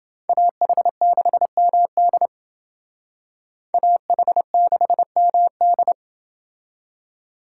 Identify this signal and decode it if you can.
Morse code, heard as a keyed tone, about 28 wpm, 700 Hz. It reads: AH6MD AH6MD